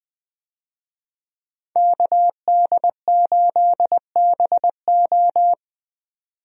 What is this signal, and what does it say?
Morse code, heard as a keyed tone, about 20 wpm, 700 Hz. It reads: KD8BO
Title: USA Call Signs - 5 Characters - 20wpm